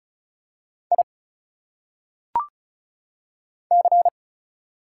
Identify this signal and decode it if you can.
Morse code, heard as a keyed tone, about 36 wpm, 700 Hz. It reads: I C
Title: Single Letters - Phonetic Pronunciation 35wpm